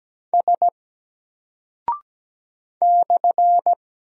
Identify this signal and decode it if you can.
Morse code, heard as a keyed tone, about 17 wpm, 700 Hz. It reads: S /
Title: Single Character (Letter:Number:Punctuation:Prosign) 17wpm